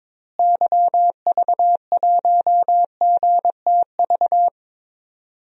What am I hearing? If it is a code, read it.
Morse code, heard as a keyed tone, about 22 wpm, 700 Hz. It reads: YV1GTV
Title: DXCC-100 Call Signs - 6 Characters - Repeated 2x - 22wpm